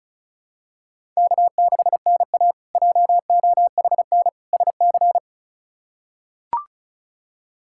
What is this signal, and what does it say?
Morse code, heard as a keyed tone, about 36 wpm, 700 Hz. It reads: K6NA JOHN SC
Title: K1USN SST QSO Practice - 35wpm